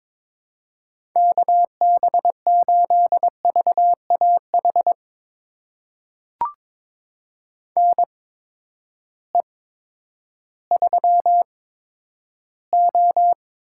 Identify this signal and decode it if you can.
Morse code, heard as a keyed tone, about 22 wpm, spaced as for 11 wpm, 700 Hz. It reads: KB8VAH NE3O
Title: USA Call Signs - 8x Character Spacing - 22wpm